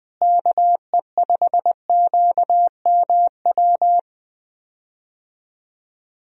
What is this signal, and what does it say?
Morse code, heard as a keyed tone, about 20 wpm, 700 Hz. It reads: KE5QMW
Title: USA Call Signs - 6 Characters - Repeated 3x - 20wpm